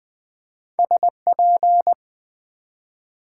Morse code, heard as a keyed tone, about 20 wpm, 700 Hz. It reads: SP
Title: Build A Word - 1 Letter At A Time - 5-Letter Words - 20wpm